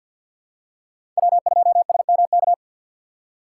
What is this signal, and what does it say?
Morse code, heard as a keyed tone, about 53 wpm, 700 Hz. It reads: W2SMK
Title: USA Call Signs - 5 Characters - 2x Character Spacing - 50wpm